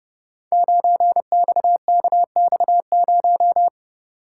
Morse code, heard as a keyed tone, about 30 wpm, 700 Hz. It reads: 9XKX0
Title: Code Groups - 5 Letter:Numbers - 30wpm